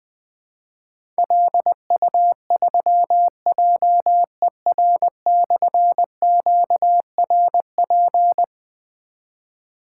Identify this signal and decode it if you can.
Morse code, heard as a keyed tone, about 20 wpm, 700 Hz. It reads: LU3JER/QRP
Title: Difficult Call Signs 20wpm